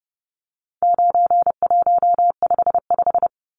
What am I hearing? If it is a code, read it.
Morse code, heard as a keyed tone, about 30 wpm, 700 Hz. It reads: 9155
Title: Numbers - 4-Digit 30wpm